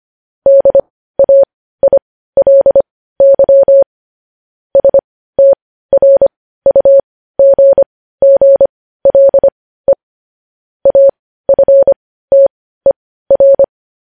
Morse code, heard as a keyed tone, about 25 wpm, spaced as for 15 wpm, 550 Hz. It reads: DAILY STRUGGLE AFTER
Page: news-headlines-in-morse-code-at-15-wpm